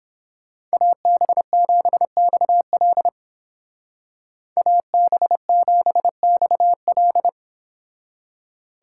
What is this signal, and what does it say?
Morse code, heard as a keyed tone, about 31 wpm, 700 Hz. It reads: AB7XL AB7XL